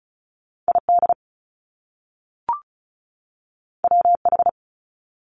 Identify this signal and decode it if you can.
Morse code, heard as a keyed tone, about 36 wpm, 700 Hz. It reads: ID WH
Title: Top Two Letter Combinations in Words 35wpm